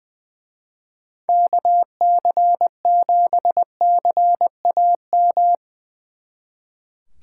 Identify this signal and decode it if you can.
Morse code, heard as a keyed tone, about 23 wpm, 700 Hz. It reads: KC7CAM
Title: Speed Racing - USA 2x3 Call Signs 20wpm